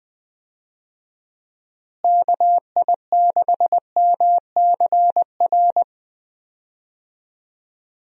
Morse code, heard as a keyed tone, about 20 wpm, 700 Hz. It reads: KI6MCR
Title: USA Call Signs - 6 Characters - Repeated 2x - 20wpm